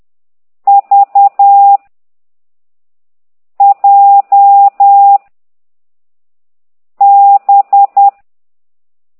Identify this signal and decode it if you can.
Morse code, heard as a keyed tone, about 10 wpm, 800 Hz. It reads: V J B
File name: exerc7